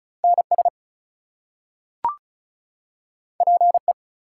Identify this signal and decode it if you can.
Morse code, heard as a keyed tone, about 36 wpm, 700 Hz. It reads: NS PE
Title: US States and Canadian Provinces 35wpm